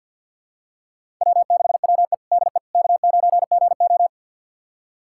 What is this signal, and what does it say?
Morse code, heard as a keyed tone, about 45 wpm, 700 Hz. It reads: W6WE DE K9GO
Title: USA Callsign DE Callsign - 4 Characters - 50wpm